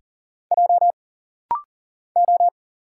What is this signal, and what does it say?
Morse code, heard as a keyed tone, about 40 wpm, 700 Hz. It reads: J O
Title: Single Letter-Number Alt - Rapid-Fire 40wpm